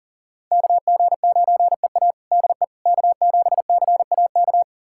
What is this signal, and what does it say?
Morse code, heard as a keyed tone, about 40 wpm, 700 Hz. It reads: KG9EA DE K7CAK
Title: USA Callsign DE Callsign - 5 Characters - 40wpm